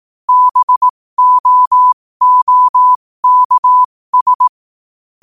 Morse code, heard as a keyed tone, about 18 wpm, spaced as for 15 wpm, 1000 Hz. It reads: BOOKS
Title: 15 WPM morse code quotes for Thu, 29 Feb 2024